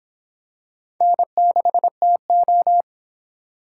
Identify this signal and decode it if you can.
Morse code, heard as a keyed tone, about 25 wpm, 700 Hz. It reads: N6TO